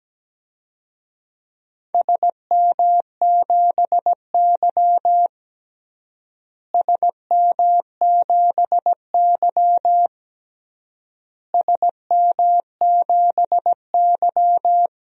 Morse code, heard as a keyed tone, about 17 wpm, 700 Hz. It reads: SM7Y SM7Y SM7Y
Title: DXCC-100 Call Signs - 4 Characters - Repeated 3x - 17wpm